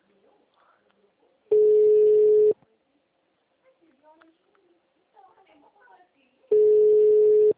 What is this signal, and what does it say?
Ringback tone (Europe, 425 Hz), with speech.